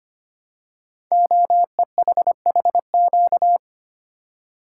Morse code, heard as a keyed tone, about 25 wpm, 700 Hz. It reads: OEHHQ
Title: Code Groups - 5 Letters - 25wpm